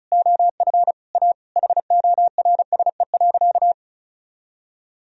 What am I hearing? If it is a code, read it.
Morse code, heard as a keyed tone, about 34 wpm, 700 Hz. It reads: OF A HORSE.
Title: Sentences from Top 1000 Words - Review Part 4 - 35wpm